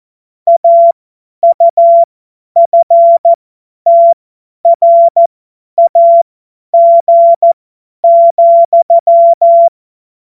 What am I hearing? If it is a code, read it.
Morse code, heard as a keyed tone, about 14 wpm, spaced as for 10 wpm, 680 Hz. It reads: AUFTRAG,
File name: Morsekurs / Witze / Witz 30. Dezember 2025 50 F70